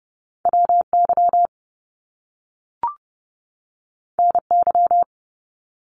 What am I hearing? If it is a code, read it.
Morse code, heard as a keyed tone, about 30 wpm, 700 Hz. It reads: WY NY
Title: US States 30wpm